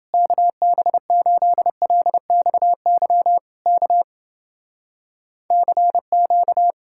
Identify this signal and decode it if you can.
Morse code, heard as a keyed tone, about 30 wpm, 700 Hz. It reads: KB8LXY K CQ